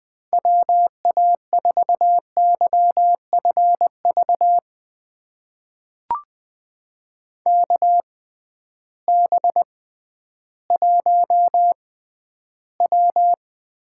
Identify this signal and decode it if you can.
Morse code, heard as a keyed tone, about 20 wpm, spaced as for 12 wpm, 700 Hz. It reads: WA4YFV KB1W
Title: USA Call Signs - 6 Characters - 6x Character Spacing - 20wpm